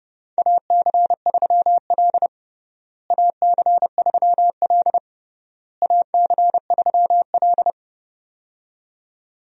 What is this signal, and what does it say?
Morse code, heard as a keyed tone, about 31 wpm, 700 Hz. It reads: AC3L AC3L AC3L
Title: USA Call Signs - 4 Characters - Repeated 3x - 30wpm